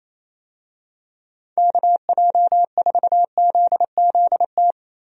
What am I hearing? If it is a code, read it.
Morse code, heard as a keyed tone, about 28 wpm, 700 Hz. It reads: KJ4ZZT
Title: USA Call Signs - 6 Characters - 28wpm